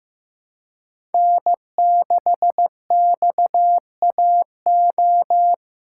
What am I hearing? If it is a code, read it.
Morse code, heard as a keyed tone, about 15 wpm, 700 Hz. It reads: N6XAO
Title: USA Call Signs - 5 Characters - 3x Character Spacing - 15wpm